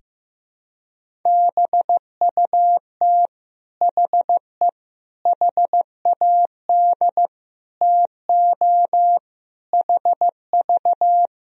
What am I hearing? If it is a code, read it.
Morse code, heard as a keyed tone, about 15 wpm, 700 Hz. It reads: BUT HE HAD TO HV
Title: Sentences from Top 100 Words and Abbreviations 15wpm